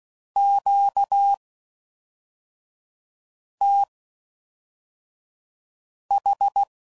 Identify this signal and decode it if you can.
Morse code, heard as a keyed tone, about 16 wpm, spaced as for 4 wpm, 800 Hz. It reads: QTH